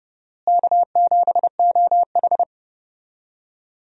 Morse code, heard as a keyed tone, about 30 wpm, 700 Hz. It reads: K7OH
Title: USA Call Signs - 4 Characters - Repeated 3x - 30wpm